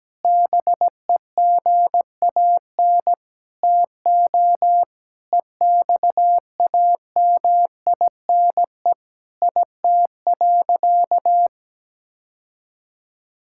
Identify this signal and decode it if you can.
Morse code, heard as a keyed tone, about 17 wpm, 700 Hz. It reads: BEGAN TO EXAMINE IT.
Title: Sentences from Top 1600 Words 17wpm